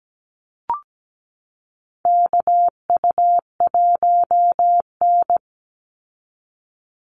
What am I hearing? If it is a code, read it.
Morse code, heard as a keyed tone, about 17 wpm, 700 Hz. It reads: KU1N
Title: Call Signs Repeated 3x - 17wpm